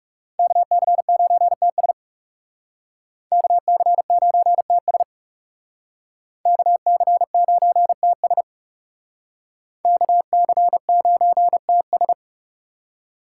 Morse code, heard as a keyed tone, about 35 wpm, 700 Hz. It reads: KC9TS KC9TS KC9TS KC9TS
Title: Speed Racing - USA 2x2 Call Signs 30wpm